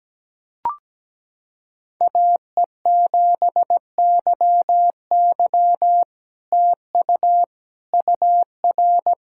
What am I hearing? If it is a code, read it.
Morse code, heard as a keyed tone, about 17 wpm, 700 Hz. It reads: AE7YY TU UR
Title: SOTA QSO Practice 17wpm